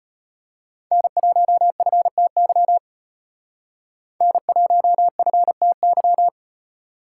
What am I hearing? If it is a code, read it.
Morse code, heard as a keyed tone, about 34 wpm, 700 Hz. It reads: N1FTY N1FTY